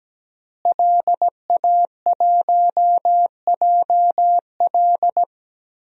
Morse code, heard as a keyed tone, about 17 wpm, 700 Hz. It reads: LA1JL
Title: DXCC-100 Call Signs - 5 Characters - Repeated 2x - 17wpm